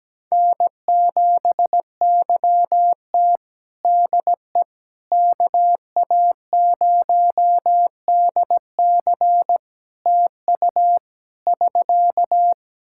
Morse code, heard as a keyed tone, about 17 wpm, 700 Hz. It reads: N7YT DE KA0DC TU <SK>
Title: Practice QSO - Exchange 4 - Ending - 17wpm